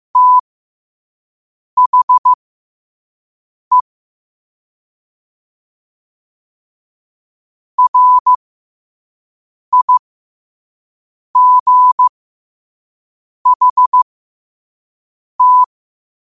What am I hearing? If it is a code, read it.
Morse code, heard as a keyed tone, about 15 wpm, spaced as for 5 wpm, 1000 Hz. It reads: THE RIGHT
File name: qotdfiles / 5wpm